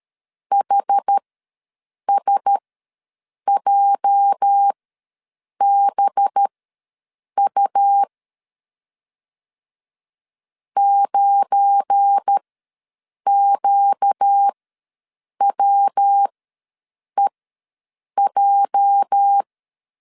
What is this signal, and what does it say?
Morse code, heard as a keyed tone, about 13 wpm, spaced as for 7 wpm, 800 Hz. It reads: HSJBU 9QWEJ
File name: LEZIONI / 8 YJ45 / 8 YJ ws 12wpm 3342